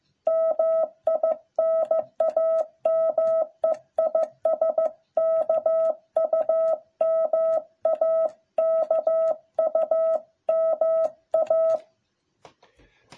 Morse code, heard as a keyed tone, about 14 wpm, 660 Hz. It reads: MI NAME IS KUMA KUMA